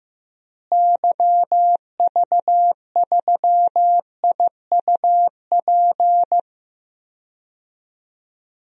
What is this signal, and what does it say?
Morse code, heard as a keyed tone, about 15 wpm, 700 Hz. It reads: YV3IUP
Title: DXCC-100 Call Signs - 6 Characters - Repeated 3x - 15wpm